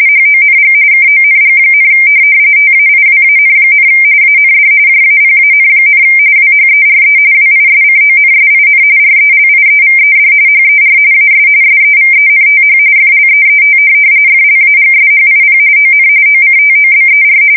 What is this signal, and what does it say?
RTTY, about 45.45 baud. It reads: HONG KONG, TAIWAN, AND JAPAN, WHERE WISCONSIN RELIEVED HER AS SEVENTH FLEET FLAGSHIP ON OCTOBER
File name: fsk170 / USS-New-Jersey